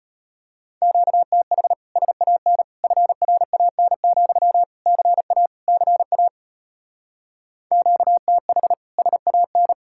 Morse code, heard as a keyed tone, about 33 wpm, 700 Hz. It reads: QTH SAN FRAN, CA CA QTH SAN